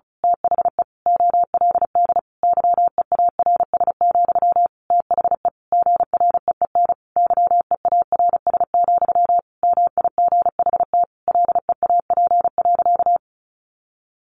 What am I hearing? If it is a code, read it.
Morse code, heard as a keyed tone, about 34 wpm, 700 Hz. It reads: THE OLD YEARS, THE GREEN YEARS, MIGHT LEAP.
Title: A Sound Of Thunder 35wpm